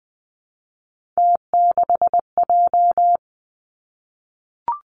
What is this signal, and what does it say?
Morse code, heard as a keyed tone, about 20 wpm, 700 Hz. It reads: T6J
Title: Three Challenging Letter-Number Combinations 20wpm